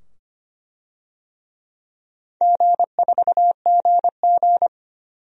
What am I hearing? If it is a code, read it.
Morse code, heard as a keyed tone, about 25 wpm, 700 Hz. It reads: G4GG
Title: DXCC-100 Call Signs - 4 Characters - 25wpm